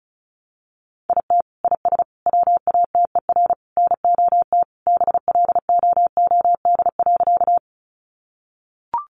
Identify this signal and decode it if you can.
Morse code, heard as a keyed tone, about 34 wpm, 700 Hz. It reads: IT IS WATER NOT BLOOD.
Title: Sentences from Top 1100 Words 35wpm